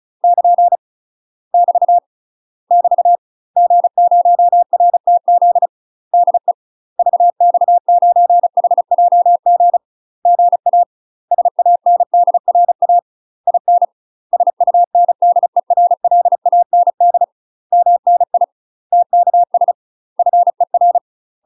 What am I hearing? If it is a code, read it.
Morse code, heard as a keyed tone, about 34 wpm, 700 Hz. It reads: <KN> <BT> <BT> G0RTZ DE VX9HJG GA SANDRA IN SUNDERLAND MNI TKS FER
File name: UK MASTER QSOs TRAINING / 35WPM / UK MASTER QSOs 0040